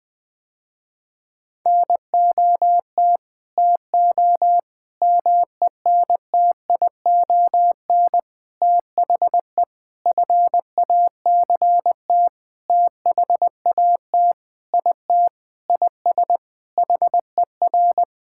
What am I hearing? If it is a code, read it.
Morse code, heard as a keyed tone, about 20 wpm, 700 Hz. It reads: NOT TO MENTION THE FACT THAT IT IS HER
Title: Sentences with 6-Word Phrases - Spoken Beginning - 20wpm